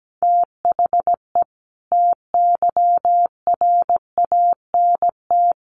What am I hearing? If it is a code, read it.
Morse code, heard as a keyed tone, about 17 wpm, 700 Hz. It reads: THE TYRANT
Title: A Sound Of Thunder 17wpm